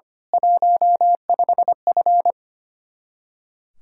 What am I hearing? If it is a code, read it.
Morse code, heard as a keyed tone, about 24 wpm, 700 Hz. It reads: 15F
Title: Speed-Racing Common QSO Phrases - 25wpm